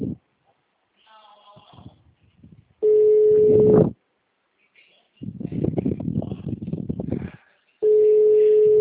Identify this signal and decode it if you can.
Ringback tone (Europe, 425 Hz).